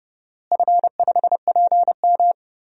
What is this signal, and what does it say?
Morse code, heard as a keyed tone, about 29 wpm, 700 Hz. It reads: F5PM
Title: DXCC-100 Call Signs - 4 Characters - Repeated 3x - 30wpm